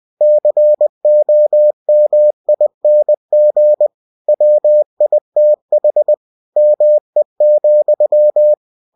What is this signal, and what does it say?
Morse code, heard as a keyed tone, about 20 wpm, 600 Hz. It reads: COMING WITH ME,
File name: cw-books / winnie-the-pooh / winnie-20wpm-page-0022